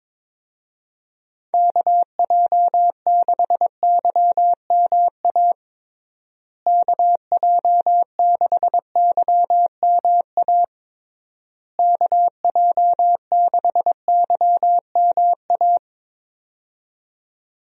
Morse code, heard as a keyed tone, about 22 wpm, 700 Hz. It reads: KJ6YMA KJ6YMA KJ6YMA
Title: USA Call Signs - 6 Characters - Repeated 3x - 22wpm